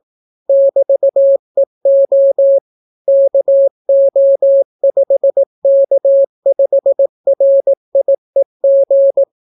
Morse code, heard as a keyed tone, about 18 wpm, 550 Hz. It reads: <BT>EO KO5K5RIEG